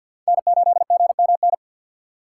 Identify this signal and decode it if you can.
Morse code, heard as a keyed tone, about 47 wpm, 690 Hz. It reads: N8GMN
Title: USA Call Signs - 2x Character Spacing - 50wpm